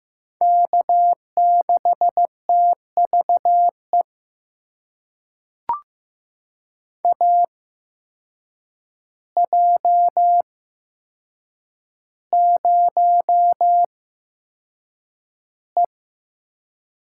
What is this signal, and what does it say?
Morse code, heard as a keyed tone, about 15 wpm, spaced as for 7 wpm, 700 Hz. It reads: K6TVE AJ0E
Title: USA Call Signs - 8x Character Spacing - 15wpm